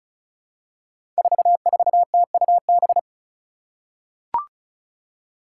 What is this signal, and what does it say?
Morse code, heard as a keyed tone, about 36 wpm, 700 Hz. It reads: 44TUB
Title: Five Challenging Letter-Number Combinations 35wpm